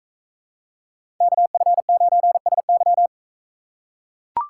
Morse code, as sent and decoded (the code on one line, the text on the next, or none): -.- ..-. ----. ... -.--
KF9SY